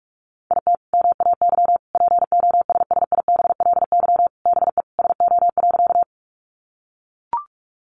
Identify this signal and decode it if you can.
Morse code, heard as a keyed tone, about 44 wpm, 700 Hz. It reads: IT MAY POSSIBLY BE SO.
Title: Sentences from Top 1300 Words 45wpm